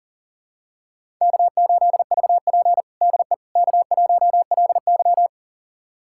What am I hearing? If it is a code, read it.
Morse code, heard as a keyed tone, about 38 wpm, 700 Hz. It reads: K8VP DE K1LY
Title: USA Callsign DE Callsign - 4 Characters - 40wpm